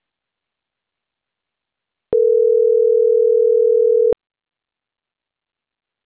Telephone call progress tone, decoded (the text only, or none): ringback tone